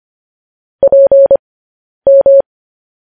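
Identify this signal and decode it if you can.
Morse code, heard as a keyed tone, about 25 wpm, spaced as for 10 wpm, 550 Hz. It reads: PM
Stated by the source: Morse code practice sending news headlines at 10 wpm.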